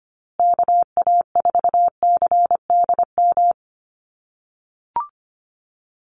Morse code, heard as a keyed tone, about 25 wpm, 700 Hz. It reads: KA4CDM
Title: USA Call Signs Repeated 2x 25wpm